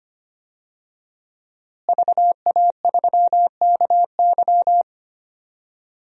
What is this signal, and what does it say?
Morse code, heard as a keyed tone, about 25 wpm, 700 Hz. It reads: VA3KY
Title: DXCC-100 Call Signs - 5 Characters - Repeated 3x - 25wpm